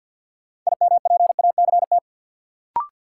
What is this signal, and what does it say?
Morse code, heard as a keyed tone, about 53 wpm, 700 Hz. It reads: IMPACT
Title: Sentences from Top 1000 Words - Review Part 6 - New Word Drill - 50wpm